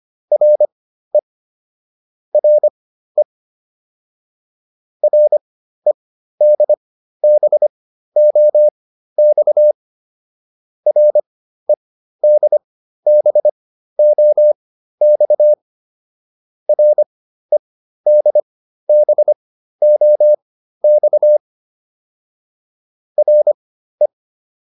Morse code, heard as a keyed tone, about 25 wpm, spaced as for 13 wpm, 600 Hz. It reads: RE RE REDBOX REDBOX REDBOX RE